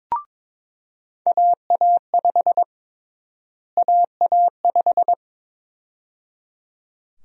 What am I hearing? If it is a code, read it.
Morse code, heard as a keyed tone, about 22 wpm, 700 Hz. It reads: AA5 AA5
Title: USA Call Sign Prefixes - Repeated 2x - 22wpm